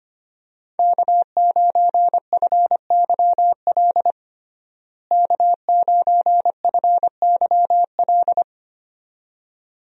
Morse code, heard as a keyed tone, about 25 wpm, 700 Hz. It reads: K9FYL K9FYL
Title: USA Call Signs - 5 Characters - Repeated 2x - 25wpm